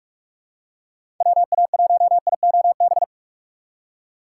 Morse code, heard as a keyed tone, about 46 wpm, 700 Hz. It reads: WA1IOB
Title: USA Call Signs - 6 Characters - 6x Character Spacing - 45wpm